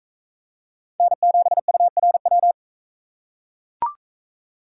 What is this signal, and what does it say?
Morse code, heard as a keyed tone, about 38 wpm, 700 Hz. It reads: N7URW